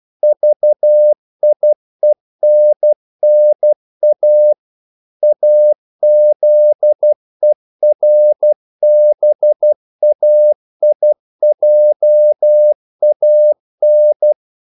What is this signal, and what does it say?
Morse code, heard as a keyed tone, about 12 wpm, 600 Hz. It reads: VIENNA AZERBAIJAN
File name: CW / Practice / NationalCapitals12